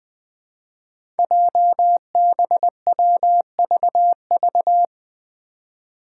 Morse code, heard as a keyed tone, about 20 wpm, 700 Hz. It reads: JBWVV
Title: Five Challenging Letter-Number Combinations 20wpm